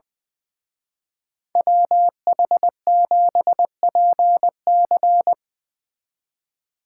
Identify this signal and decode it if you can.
Morse code, heard as a keyed tone, about 22 wpm, 700 Hz. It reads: WH7PC
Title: Speed Racing - USA 2x2 Call Signs 20wpm